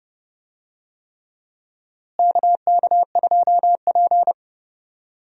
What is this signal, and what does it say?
Morse code, heard as a keyed tone, about 29 wpm, 700 Hz. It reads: KK2P
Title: USA Call Signs - 4 Characters - Repeated 3x - 30wpm